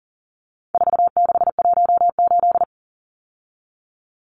Morse code, heard as a keyed tone, about 40 wpm, 700 Hz. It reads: 4618
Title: Numbers - 4-Digit 40wpm